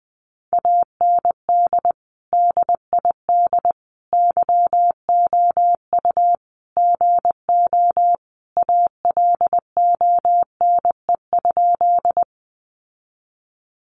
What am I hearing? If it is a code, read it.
Morse code, heard as a keyed tone, about 20 wpm, 700 Hz. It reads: AND DID YOU GO ALONE?
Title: Sentences from Top 1000 Words 20wpm